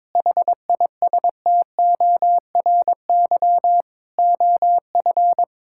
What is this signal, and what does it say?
Morse code, heard as a keyed tone, about 22 wpm, 700 Hz. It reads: HISTORY OF
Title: Sentences from Top 1000 Words - Review Part 5 - 22wpm